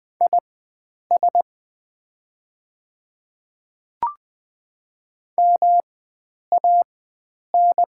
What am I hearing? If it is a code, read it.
Morse code, heard as a keyed tone, about 20 wpm, spaced as for 9 wpm, 700 Hz. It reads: IS MAN
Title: 4x Character Spacing - Top 300 Words - 20wpm